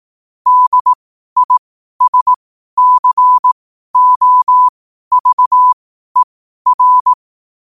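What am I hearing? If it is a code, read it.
Morse code, heard as a keyed tone, about 18 wpm, spaced as for 12 wpm, 1000 Hz. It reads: DISCOVER
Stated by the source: Morse code at 12 WPM.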